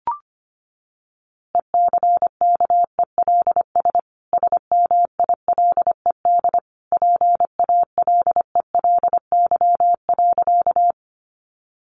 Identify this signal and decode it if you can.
Morse code, heard as a keyed tone, about 25 wpm, 700 Hz. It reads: ECKELS SMILED PALELY.
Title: A Sound Of Thunder 25wpm